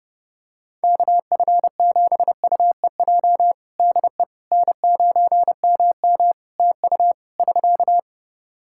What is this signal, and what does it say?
Morse code, heard as a keyed tone, about 30 wpm, 700 Hz. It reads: KF7UEJ DE N9MM TU <SK>
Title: Practice QSO - Exchange 4 - Ending - 30wpm